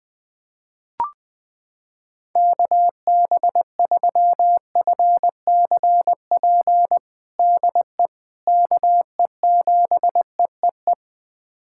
Morse code, heard as a keyed tone, about 20 wpm, 700 Hz. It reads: KB3FCP DE KE7EEE
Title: USA Callsign DE Callsign 20wpm